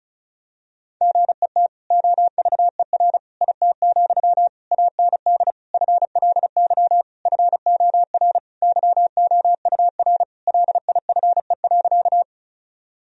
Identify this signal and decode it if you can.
Morse code, heard as a keyed tone, about 36 wpm, 700 Hz. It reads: GET OVER IT, AND FLY FOR YOUR LIFE.
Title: Sentences from Top 1000 Words - Review Part 2 - 35wpm